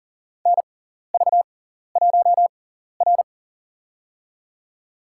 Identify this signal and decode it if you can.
Morse code, heard as a keyed tone, about 40 wpm, spaced as for 14 wpm, 700 Hz. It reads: NV1R
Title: USA Call Signs - 4 Characters - 6x Character Spacing - 40wpm